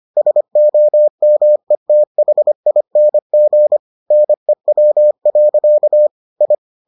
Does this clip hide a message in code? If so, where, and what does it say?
Yes: Morse code from 0.2 to 6.5 s, heard as a keyed tone, about 25 wpm, 600 Hz. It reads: SOMETHING NEW. I